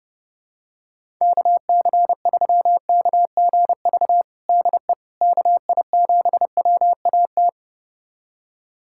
Morse code, heard as a keyed tone, about 32 wpm, 700 Hz. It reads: KC3KGV DE KI7WAT
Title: USA Callsign DE Callsign 30wpm